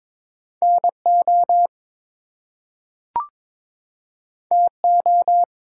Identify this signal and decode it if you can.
Morse code, heard as a keyed tone, about 22 wpm, 700 Hz. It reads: NO TO
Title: Two Letter Words - 22wpm